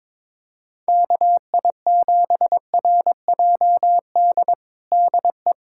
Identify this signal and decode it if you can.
Morse code, heard as a keyed tone, about 22 wpm, 700 Hz. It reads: KI7RJD DE
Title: USA Callsign DE Callsign - 6 Characters - 22wpm